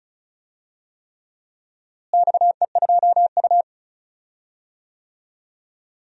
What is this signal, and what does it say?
Morse code, heard as a keyed tone, about 36 wpm, 700 Hz. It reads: XE2U
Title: DXCC-100 Call Signs - 4 Characters - Repeated 3x - 35wpm